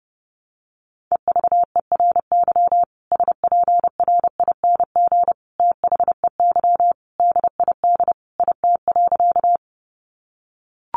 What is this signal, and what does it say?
Morse code, heard as a keyed tone, about 30 wpm, 700 Hz. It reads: EVERY SPRING THEY DID IT.
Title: Sentences from Top 1200 Words 30wpm